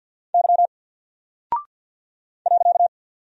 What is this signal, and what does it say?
Morse code, heard as a keyed tone, about 48 wpm, 700 Hz. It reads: Y .
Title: Single Character (Letter:Number:Punctuation:Prosign) Alt 50wpm